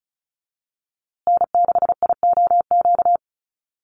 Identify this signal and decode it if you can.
Morse code, heard as a keyed tone, about 34 wpm, 700 Hz. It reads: N6IOQ
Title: USA Call Signs Repeated 3x 35wpm